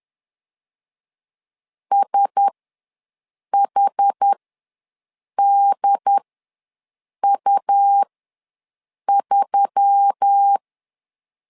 Morse code, heard as a keyed tone, about 10 wpm, spaced as for 6 wpm, 800 Hz. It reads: SHDU3
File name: LEZIONI / 4 Q1S3 ws 10wpm 2437